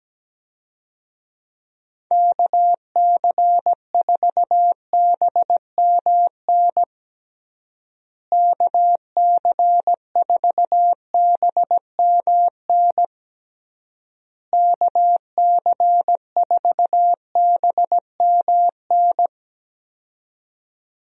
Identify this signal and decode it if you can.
Morse code, heard as a keyed tone, about 17 wpm, 700 Hz. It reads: KC4BMN KC4BMN KC4BMN
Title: USA Call Signs - 6 Characters - Repeated 3x - 17wpm